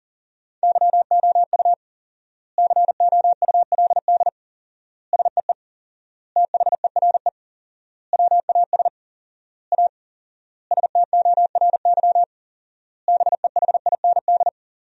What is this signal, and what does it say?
Morse code, heard as a keyed tone, about 40 wpm, 700 Hz. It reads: YOU COULD SEE THERE WAS A STORY BEHIND
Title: 4x Word-Spacing - Sentences from Top 500 Words - 40wpm